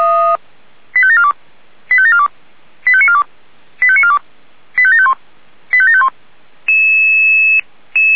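Five-tone selective call (ZVEI), heard as quick runs of five tones, tones of about 70 ms. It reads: 85742 85742 85942 85942 85751 85751
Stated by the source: ZVEI five-tone selective call received on an SDR.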